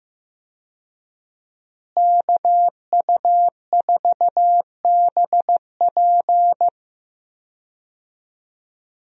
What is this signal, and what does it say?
Morse code, heard as a keyed tone, about 15 wpm, 700 Hz. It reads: KU4BP
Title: USA Call Signs - 5 Characters - Repeated 2x - 15wpm